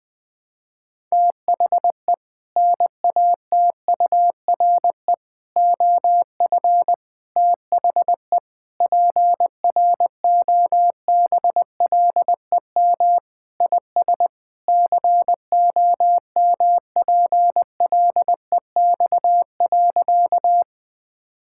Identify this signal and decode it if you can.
Morse code, heard as a keyed tone, about 21 wpm, 700 Hz. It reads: THE NATURE OF THE PROBLEM IS COMPLEX.
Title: Sentences with 4-Word Phrases - 20wpm